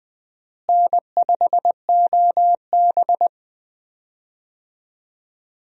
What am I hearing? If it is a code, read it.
Morse code, heard as a keyed tone, about 20 wpm, 700 Hz. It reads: N5OB
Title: USA Call Signs - 4 Characters - Repeated 2x - 20wpm